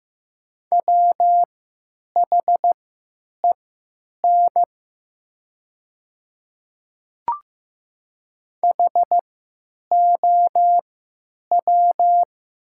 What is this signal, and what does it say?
Morse code, heard as a keyed tone, about 15 wpm, spaced as for 9 wpm, 700 Hz. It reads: WHEN HOW
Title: 3x Character Spacing - Top 100 Words - 15wpm